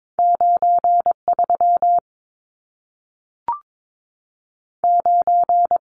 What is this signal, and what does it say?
Morse code, heard as a keyed tone, about 22 wpm, 700 Hz. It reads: 9 3 9
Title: Numbers - 2-Digit 22wpm